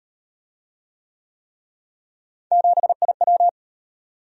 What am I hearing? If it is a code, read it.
Morse code, heard as a keyed tone, about 34 wpm, 700 Hz. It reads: 7IW